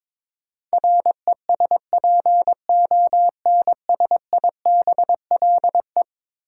Morse code, heard as a keyed tone, about 22 wpm, 700 Hz. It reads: RESPONSIBLE
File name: practice / 22wpm / Eleven Letter Words - 22wpm